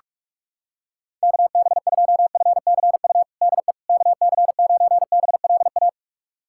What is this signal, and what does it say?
Morse code, heard as a keyed tone, about 45 wpm, 700 Hz. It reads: KB2FCU DE KC9BLA
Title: USA Callsign DE Callsign 45wpm